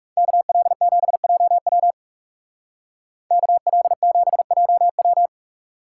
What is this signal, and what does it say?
Morse code, heard as a keyed tone, about 36 wpm, 700 Hz. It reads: KL7JW KL7JW
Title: Speed Racing - USA 2x2 Call Signs 30wpm